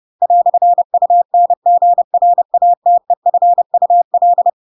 Morse code, heard as a keyed tone, about 30 wpm, 700 Hz. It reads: "UNGRATEFUL
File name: Pinocchio / Pinocchio ch3 30 30 ik1hge